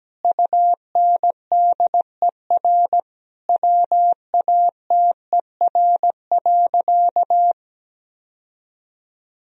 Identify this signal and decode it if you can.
Morse code, heard as a keyed tone, about 17 wpm, 700 Hz. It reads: UNDER WATER.
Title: Sentences from Top 1000 Words - Review Part 2 - 17wpm